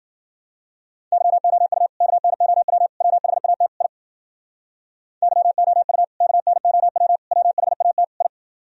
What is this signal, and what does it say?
Morse code, heard as a keyed tone, about 51 wpm, 700 Hz. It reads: YOU KNOW WHAT I YOU KNOW WHAT I